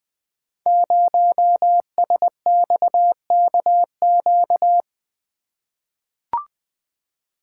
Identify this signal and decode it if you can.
Morse code, heard as a keyed tone, about 20 wpm, 700 Hz. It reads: 0SXKQ
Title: Code Groups - 5 Letter:Numbers - 20wpm